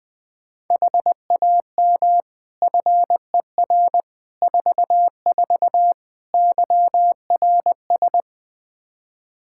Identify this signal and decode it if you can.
Morse code, heard as a keyed tone, about 20 wpm, 700 Hz. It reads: HAM FER 44 YRS
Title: K1USN SST QSO Practice 20wpm